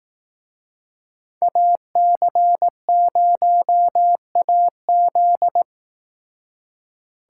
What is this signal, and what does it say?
Morse code, heard as a keyed tone, about 18 wpm, 700 Hz. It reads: AC0AZ
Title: Speed Racing - USA 2x2 Call Signs 15wpm